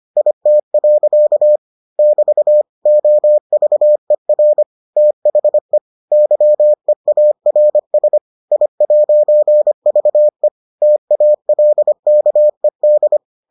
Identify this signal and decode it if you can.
Morse code, heard as a keyed tone, about 25 wpm, 600 Hz. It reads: IT. <BT> OVER THE YEARS I'VE TALKED